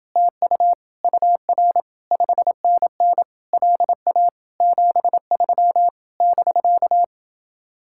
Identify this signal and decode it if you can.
Morse code, heard as a keyed tone, about 29 wpm, 700 Hz. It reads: TU UR 5NN LA 73 <BK>